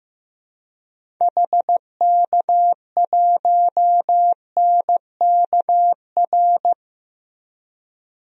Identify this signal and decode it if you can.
Morse code, heard as a keyed tone, about 15 wpm, 700 Hz. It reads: HK1NKR
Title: DXCC-100 Call Signs - 6 Characters - Repeated 2x - 15wpm